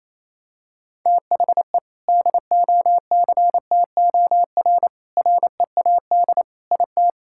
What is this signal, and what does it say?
Morse code, heard as a keyed tone, about 28 wpm, 700 Hz. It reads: THE DOCTOR READ IT
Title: Sentences from Top 1600 Words 28wpm